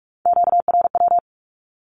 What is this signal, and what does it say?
Morse code, heard as a keyed tone, about 46 wpm, 700 Hz. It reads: QRW
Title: Q-Codes with Spoken Meaning - 45wpm